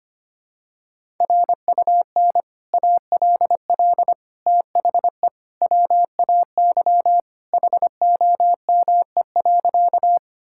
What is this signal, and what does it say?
Morse code, heard as a keyed tone, about 25 wpm, 700 Hz. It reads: RUN ALL THE WAY HOME.
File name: practice / 25wpm / Sentences with 3-Word Phrases - 25wpm